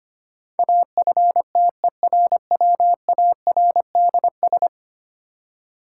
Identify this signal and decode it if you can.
Morse code, heard as a keyed tone, about 25 wpm, 700 Hz. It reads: AFTERWARDS
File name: 10 Letter Words - First 3 Letters Given - 25wpm